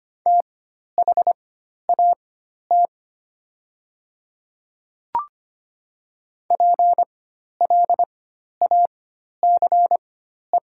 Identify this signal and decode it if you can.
Morse code, heard as a keyed tone, about 25 wpm, spaced as for 12 wpm, 700 Hz. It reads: THAT PLACE
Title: 4x Character Spacing - Top 300 Words - 25wpm